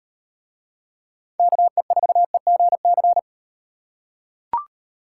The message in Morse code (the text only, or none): KE4EGC